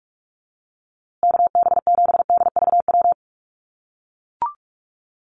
Morse code, heard as a keyed tone, about 44 wpm, 700 Hz. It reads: KB7DVW
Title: USA Call Signs Repeated 3x 45wpm